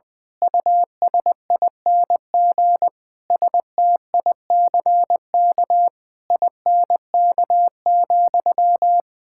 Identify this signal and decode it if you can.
Morse code, heard as a keyed tone, about 21 wpm, 700 Hz. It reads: USING STICK INK,
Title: Pangram Challenge v2 - 20wpm